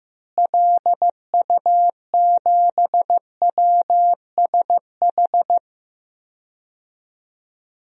Morse code, heard as a keyed tone, about 16 wpm, 700 Hz. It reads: LU7WSH
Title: DXCC-100 Call Signs - 6 Characters - Repeated 3x - 15wpm